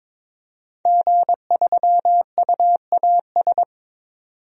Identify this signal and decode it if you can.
Morse code, heard as a keyed tone, about 22 wpm, 700 Hz. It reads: G3UAS
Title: DXCC-100 Call Signs - 5 Characters - 22wpm